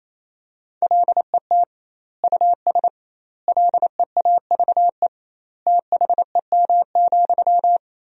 Morse code, heard as a keyed tone, about 28 wpm, 700 Hz. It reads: LET US LEAVE THEM,
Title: 2x Word-Spacing - Sentences from Top 400 Words - 28wpm